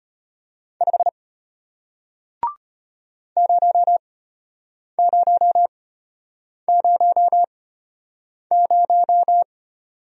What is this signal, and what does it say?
Morse code, heard as a keyed tone, about 33 wpm, spaced as for 8 wpm, 700 Hz. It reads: 5 0000